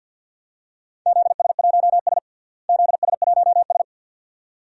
Morse code, heard as a keyed tone, about 52 wpm, 700 Hz. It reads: ZS1S ZS1S
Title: DXCC-100 Call Signs - 4 Characters - Repeated 2x - 50wpm